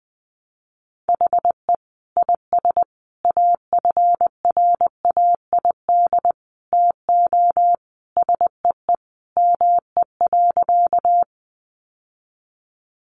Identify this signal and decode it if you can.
Morse code, heard as a keyed tone, about 20 wpm, 700 Hz. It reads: HE IS AFRAID TO SEE ME.
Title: Sentences from Top 1500 Words 20wpm